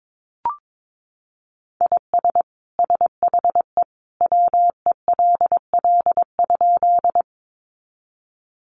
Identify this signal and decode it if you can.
Morse code, heard as a keyed tone, about 22 wpm, 700 Hz. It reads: IS SHE WELL?
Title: Sentences from Top 200 Words 22wpm